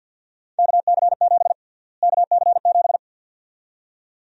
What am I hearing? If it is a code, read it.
Morse code, heard as a keyed tone, about 52 wpm, 700 Hz. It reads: KC7 KC7
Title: USA Call Sign Prefixes - Repeated 2x - 50wpm